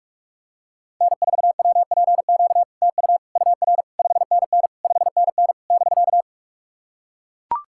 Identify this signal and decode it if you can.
Morse code, heard as a keyed tone, about 44 wpm, 690 Hz. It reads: N4WPQ TU UR 5NN 5NN <BK>
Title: SOTA QSO Practice 45wpm